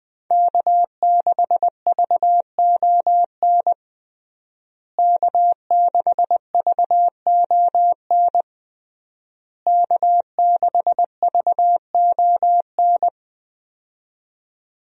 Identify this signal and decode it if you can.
Morse code, heard as a keyed tone, about 20 wpm, 700 Hz. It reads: K6VON K6VON K6VON
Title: USA Call Signs - 5 Characters - Repeated 3x - 20wpm